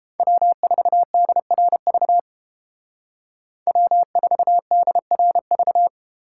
Morse code, heard as a keyed tone, about 31 wpm, 700 Hz. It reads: W4DRV W4DRV